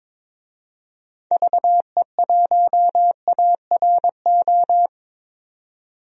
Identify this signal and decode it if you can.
Morse code, heard as a keyed tone, about 22 wpm, 700 Hz. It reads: VE1ARO
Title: DXCC-100 Call Signs - 6 Characters - 22wpm